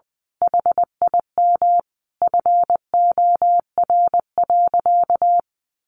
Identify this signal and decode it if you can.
Morse code, heard as a keyed tone, about 20 wpm, 700 Hz. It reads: HIM FOR.
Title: Sentences from Top 1000 Words 20wpm